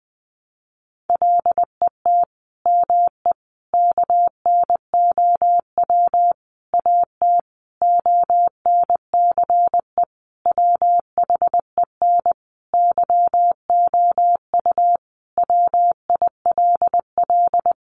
Morse code, heard as a keyed tone, about 20 wpm, 700 Hz. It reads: LET ME KNOW AT ONCE WHEN YOU WILL
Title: Sentences from Top 300 Words 20wpm